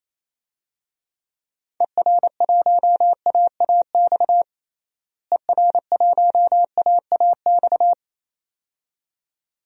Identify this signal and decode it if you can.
Morse code, heard as a keyed tone, about 28 wpm, 700 Hz. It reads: ER1AAX ER1AAX
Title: DXCC-100 Call Signs - 6 Characters - Repeated 2x - 28wpm